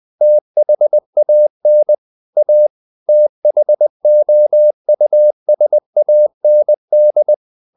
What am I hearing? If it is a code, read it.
Morse code, heard as a keyed tone, about 20 wpm, 600 Hz. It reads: THAN A THOUSAND
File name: the old man and the sea / the old man and the sea part 1 20wpm